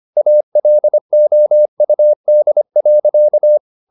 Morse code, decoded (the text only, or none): ALOUD.